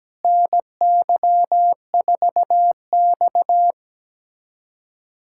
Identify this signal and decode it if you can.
Morse code, heard as a keyed tone, about 17 wpm, 700 Hz. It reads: NY4X
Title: USA Callsign DE Callsign - 4 Characters - 17wpm